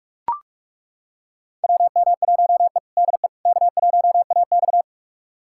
Morse code, heard as a keyed tone, about 41 wpm, 700 Hz. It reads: WM1E DE K1AX
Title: USA Callsign DE Callsign - 4 Characters - 45wpm